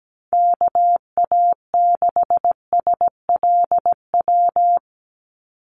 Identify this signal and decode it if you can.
Morse code, heard as a keyed tone, about 17 wpm, 700 Hz. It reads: KA6SLW
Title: USA Call Signs 17wpm